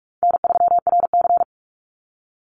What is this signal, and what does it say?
Morse code, heard as a keyed tone, about 44 wpm, 700 Hz. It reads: N3RC
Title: Easy Call Signs Repeated 2x - 45wpm